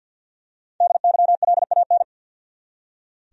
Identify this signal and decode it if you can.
Morse code, heard as a keyed tone, about 46 wpm, 700 Hz. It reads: DYLAN
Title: USA Common Names - 50wpm